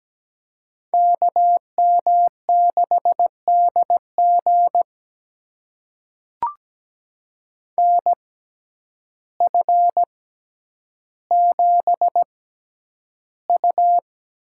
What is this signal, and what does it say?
Morse code, heard as a keyed tone, about 17 wpm, spaced as for 9 wpm, 700 Hz. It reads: KM6DG NF7U
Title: USA Call Signs - 6x Character Spacing - 17wpm